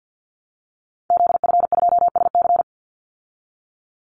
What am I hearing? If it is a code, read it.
Morse code, heard as a keyed tone, about 50 wpm, 700 Hz. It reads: ZF2SC
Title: Call Signs Repeated 3x - 50wpm